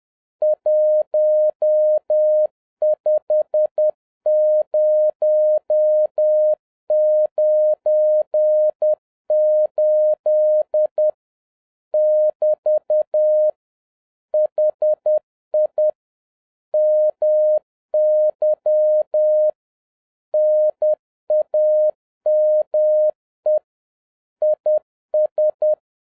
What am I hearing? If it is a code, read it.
Morse code, heard as a keyed tone, about 10 wpm, 610 Hz. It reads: 15098 <BT> HI MY NAME IS